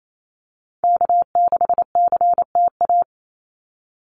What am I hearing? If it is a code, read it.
Morse code, heard as a keyed tone, about 28 wpm, 700 Hz. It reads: K6CTA
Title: Call Signs Repeated 2x - 28wpm